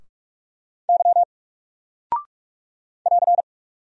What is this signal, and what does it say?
Morse code, heard as a keyed tone, about 44 wpm, 700 Hz. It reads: Y <AR>
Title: Single Character (Letter:Number:Punctuation:Prosign) Alt 45wpm